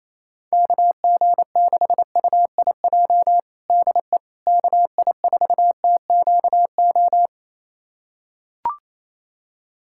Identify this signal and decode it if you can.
Morse code, heard as a keyed tone, about 27 wpm, 700 Hz. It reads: KG6UIJ DE KI4TQO
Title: USA Callsign DE Callsign - 6 Characters - 28wpm